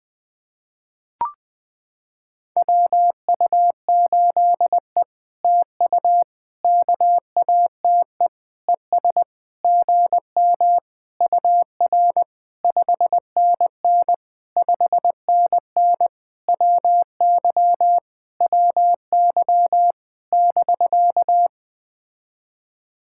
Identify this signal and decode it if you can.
Morse code, heard as a keyed tone, about 20 wpm, 700 Hz. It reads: WU8E TU KATE ES GM UR 5NN 5NN WY WY <BK>
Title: POTA QSO Practice - 20wpm